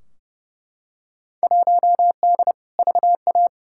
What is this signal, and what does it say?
Morse code, heard as a keyed tone, about 32 wpm, 700 Hz. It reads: 1D VA
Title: Field Day Exchanges - 30wpm